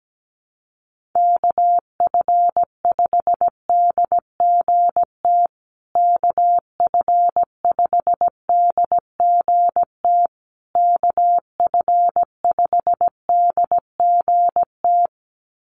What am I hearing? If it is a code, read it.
Morse code, heard as a keyed tone, about 17 wpm, 700 Hz. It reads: KF5DGT KF5DGT KF5DGT
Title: USA Call Signs Repeated 3x 17wpm